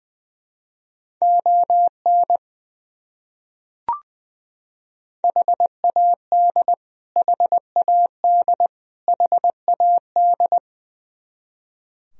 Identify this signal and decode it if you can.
Morse code, heard as a keyed tone, about 20 wpm, 700 Hz. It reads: ON HAD HAD HAD
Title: Top 100 Words Repeated 20wpm